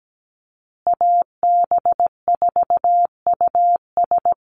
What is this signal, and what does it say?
Morse code, heard as a keyed tone, about 17 wpm, 700 Hz. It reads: AB4US
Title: USA Call Signs Repeated 2x 17wpm